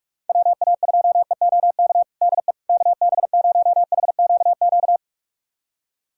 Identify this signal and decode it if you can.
Morse code, heard as a keyed tone, about 46 wpm, 700 Hz. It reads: WA2EOK DE KB0HQQ
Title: USA Callsign DE Callsign - 6 Characters - 45wpm